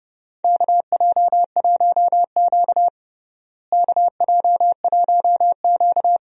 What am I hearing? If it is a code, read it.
Morse code, heard as a keyed tone, about 30 wpm, 700 Hz. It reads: KJ1Q KJ1Q
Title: USA Call Signs - 4 Characters - Repeated 2x - 30wpm